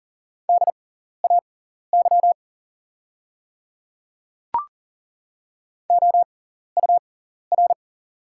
Morse code, heard as a keyed tone, about 40 wpm, spaced as for 14 wpm, 700 Hz. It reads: DAY OUR
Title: 6x Character Spacing - Top 200 Words - 40wpm